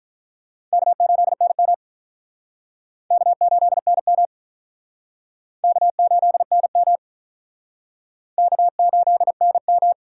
Morse code, heard as a keyed tone, about 47 wpm, 700 Hz. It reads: K8NM K8NM K8NM K8NM